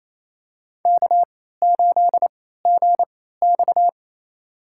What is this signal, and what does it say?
Morse code, heard as a keyed tone, about 28 wpm, spaced as for 16 wpm, 700 Hz. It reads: K8GX
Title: USA Call Signs - 4 Characters - 3x Character Spacing - 28wpm